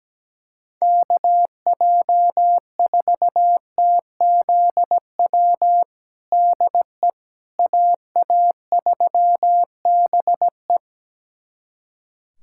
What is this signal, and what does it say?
Morse code, heard as a keyed tone, about 17 wpm, 700 Hz. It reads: KJ4TZW DE AA3BE
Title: USA Callsign DE Callsign 17wpm